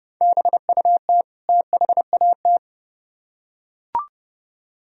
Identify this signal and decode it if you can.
Morse code, heard as a keyed tone, about 29 wpm, 700 Hz. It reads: BUT THAT